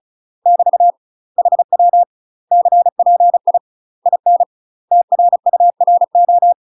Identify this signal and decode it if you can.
Morse code, heard as a keyed tone, about 34 wpm, 700 Hz. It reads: <BT> HW CPI IN TRURO
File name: UK MASTER QSOs TRAINING / 35WPM / UK MASTER QSOs 0030